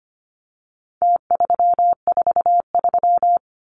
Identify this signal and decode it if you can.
Morse code, heard as a keyed tone, about 25 wpm, 700 Hz. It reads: T343
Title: Numbers - Contest Style - 4-Digit 25wpm